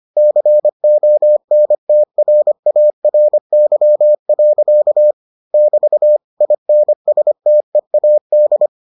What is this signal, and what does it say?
Morse code, heard as a keyed tone, about 25 wpm, 600 Hz. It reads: CONTRARY. <BT> INSTEAD